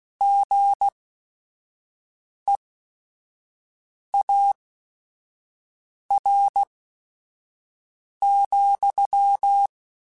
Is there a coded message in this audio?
Yes: Morse code from 0.2 to 9.7 s, heard as a keyed tone, about 16 wpm, spaced as for 5 wpm, 800 Hz. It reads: GEAR,